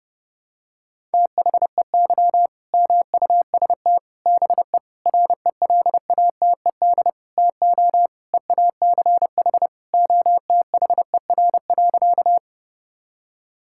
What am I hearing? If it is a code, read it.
Morse code, heard as a keyed tone, about 32 wpm, 700 Hz. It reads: THEY MUST BE RELATED TO EACH OTHER.
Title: Sentences from Top 1000 Words - Review Part 5 - 30wpm